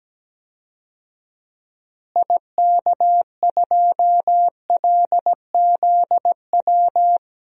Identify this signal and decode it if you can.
Morse code, heard as a keyed tone, about 17 wpm, 700 Hz. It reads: IK2LZW
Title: DXCC-100 Call Signs - 6 Characters - Repeated 2x - 17wpm